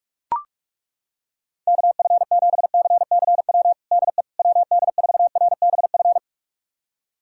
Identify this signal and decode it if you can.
Morse code, heard as a keyed tone, about 41 wpm, 700 Hz. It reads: KF7CCW DE WD4RBF
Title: USA Callsign DE Callsign - 6 Characters - 45wpm